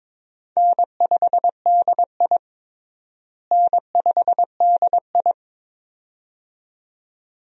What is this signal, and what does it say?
Morse code, heard as a keyed tone, about 21 wpm, 700 Hz. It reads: N5DI N5DI
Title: USA Call Signs - 4 Characters - Repeated 2x - 22wpm